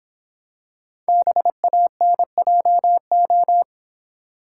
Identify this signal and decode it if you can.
Morse code, heard as a keyed tone, about 26 wpm, 700 Hz. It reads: BANJO